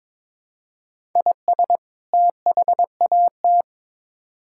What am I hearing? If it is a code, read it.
Morse code, heard as a keyed tone, about 21 wpm, 700 Hz. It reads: IS THAT
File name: Build A Phrase - 2-Word Phrases - 22wpm